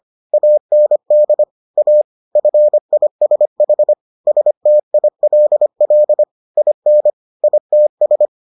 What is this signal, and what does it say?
Morse code, heard as a keyed tone, about 25 wpm, 600 Hz. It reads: AND A FISH STILL IN ITS